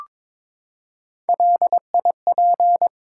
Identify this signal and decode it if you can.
Morse code, heard as a keyed tone, about 23 wpm, 700 Hz. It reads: LIP
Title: Three Letter Words - 22wpm